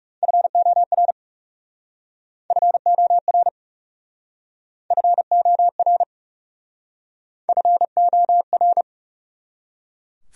Morse code, heard as a keyed tone, about 37 wpm, 700 Hz. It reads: FOR FOR FOR FOR
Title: Speed-Racing Top 100 Words 30wpm